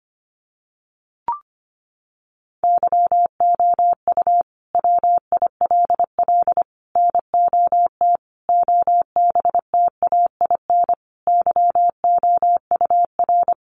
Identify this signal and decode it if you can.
Morse code, heard as a keyed tone, about 25 wpm, 700 Hz. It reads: YOU WILL NOT OBTAIN YOUR
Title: Sentences from Top 1300 Words 25wpm